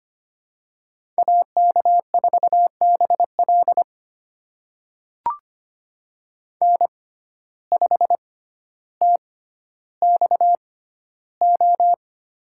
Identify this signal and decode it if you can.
Morse code, heard as a keyed tone, about 25 wpm, spaced as for 13 wpm, 700 Hz. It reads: AK4BL N5TXO
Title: USA Call Signs - 5 Characters - 6x Character Spacing - 25wpm